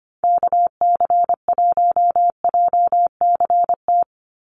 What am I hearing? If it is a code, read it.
Morse code, heard as a keyed tone, about 25 wpm, 700 Hz. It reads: KC1JCT
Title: USA Call Signs 25wpm